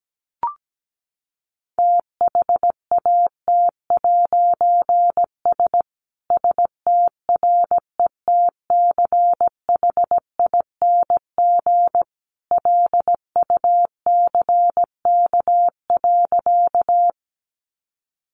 Morse code, heard as a keyed tone, about 17 wpm, 700 Hz. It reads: THAT'S STRETCHING LUCK.
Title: A Sound Of Thunder 17wpm